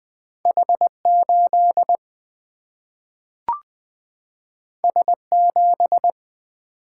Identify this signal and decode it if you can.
Morse code, heard as a keyed tone, about 20 wpm, 700 Hz. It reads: H8 S7
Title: Single-Letter Single-Number 20wpm